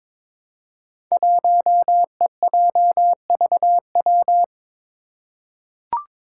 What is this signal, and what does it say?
Morse code, heard as a keyed tone, about 22 wpm, 700 Hz. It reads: 1EJVW
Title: Five Challenging Letter-Number Combinations 22wpm